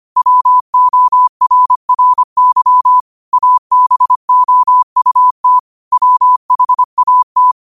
Morse code, heard as a keyed tone, about 25 wpm, 1000 Hz. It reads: WORRY ABOUT WHAT
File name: 25wpm